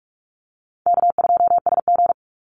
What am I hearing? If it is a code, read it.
Morse code, heard as a keyed tone, about 44 wpm, 700 Hz. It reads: K2SG
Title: Easy Call Signs Repeated 2x - 45wpm